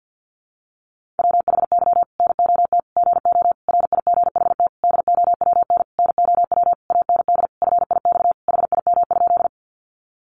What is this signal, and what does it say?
Morse code, heard as a keyed tone, about 50 wpm, 700 Hz. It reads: WHY NOT GO RIGHT DOWN NOW AND FIX HIM?
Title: Sentences from Top 1400 Words 50wpm